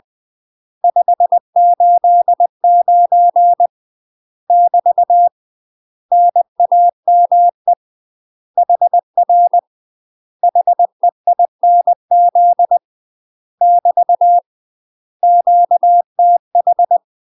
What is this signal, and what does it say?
Morse code, heard as a keyed tone, about 20 wpm, 700 Hz. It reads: 589 <BT> NAME HR HEINZ <BT> QTH
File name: UK MASTER QSOs TRAINING / 20WPM / UK MASTER QSOs 0058